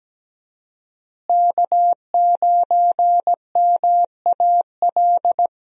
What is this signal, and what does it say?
Morse code, heard as a keyed tone, about 17 wpm, 700 Hz. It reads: K9MAL
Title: USA Call Signs - 5 Characters - Repeated 3x - 17wpm